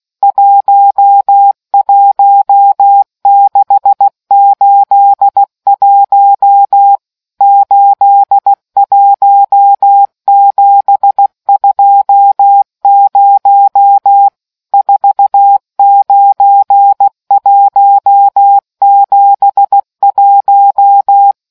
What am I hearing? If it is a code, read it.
Morse code, heard as a keyed tone, about 16 wpm, 800 Hz. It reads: 11681 81720 49171